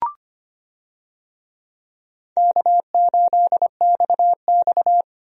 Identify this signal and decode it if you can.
Morse code, heard as a keyed tone, about 25 wpm, 700 Hz. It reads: K8XX
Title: USA Call Signs - 4 Characters - 25wpm